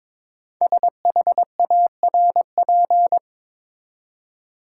Morse code, heard as a keyed tone, about 22 wpm, 700 Hz. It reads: SHARP
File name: 5 Letter Words - First 3 Letters Given - 22wpm